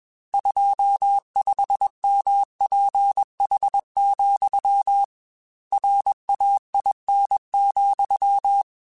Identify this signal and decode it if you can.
Morse code, heard as a keyed tone, about 21 wpm, 800 Hz. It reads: 25 MPH, RAIN,
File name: Weather Sets / Weather Sets-mp3s / Weather 5 5 @20